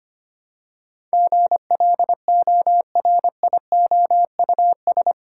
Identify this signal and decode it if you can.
Morse code, heard as a keyed tone, about 25 wpm, 700 Hz. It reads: GLORIOUS
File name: 8 Letter Words - First 3 Letters Given - 25wpm